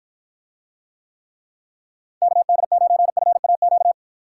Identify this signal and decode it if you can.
Morse code, heard as a keyed tone, about 53 wpm, 700 Hz. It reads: KD9FAQ